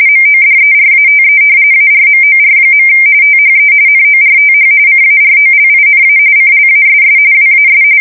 RTTY, about 45.45 baud. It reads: MODERATE ARMOR. THE NORTH CAROLINA CLASS WAS
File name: fsk170 / battleships